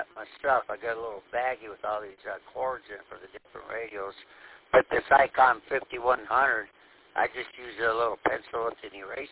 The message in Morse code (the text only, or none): none